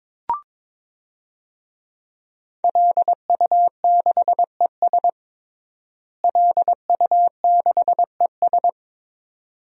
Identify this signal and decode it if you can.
Morse code, heard as a keyed tone, about 22 wpm, 700 Hz. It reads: LU6ES LU6ES
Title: DXCC-100 Call Signs - 5 Characters - Repeated 2x - 22wpm